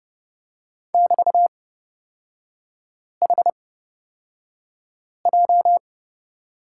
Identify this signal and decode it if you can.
Morse code, heard as a keyed tone, about 30 wpm, spaced as for 5 wpm, 700 Hz. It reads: <BT>HJ
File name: Lesson 36e - BT - Three Characters 30wpm